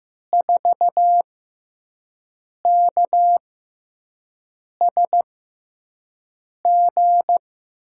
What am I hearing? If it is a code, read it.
Morse code, heard as a keyed tone, about 15 wpm, spaced as for 5 wpm, 700 Hz. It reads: 4KSG
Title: USA Call Signs - 6 Characters - 6x Character Spacing - 15wpm